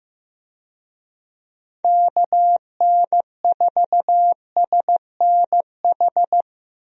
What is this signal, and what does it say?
Morse code, heard as a keyed tone, about 16 wpm, 700 Hz. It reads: KN4SNH
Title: USA Call Signs - 6 Characters - Repeated 2x - 15wpm